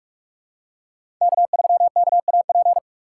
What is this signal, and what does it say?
Morse code, heard as a keyed tone, about 46 wpm, 700 Hz. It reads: K3KAP
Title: USA Call Signs - 6x Character Spacing - 45wpm